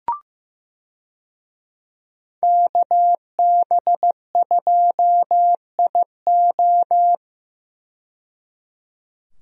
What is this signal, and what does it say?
Morse code, heard as a keyed tone, about 15 wpm, 700 Hz. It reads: KB2IO
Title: USA Call Signs - 5 Characters - 15wpm